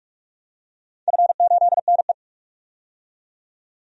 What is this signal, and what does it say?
Morse code, heard as a keyed tone, about 44 wpm, 700 Hz. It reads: F8NE
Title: DXCC-100 Call Signs - 4 Characters - 45wpm